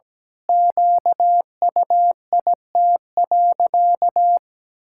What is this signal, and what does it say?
Morse code, heard as a keyed tone, about 17 wpm, 700 Hz. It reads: QUIT.
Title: Pangram Challenge v2 - 17wpm